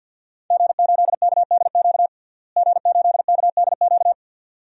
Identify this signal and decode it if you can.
Morse code, heard as a keyed tone, about 52 wpm, 700 Hz. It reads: G8KDQ G8KDQ
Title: DXCC-100 Call Signs - 5 Characters - Repeated 2x - 50wpm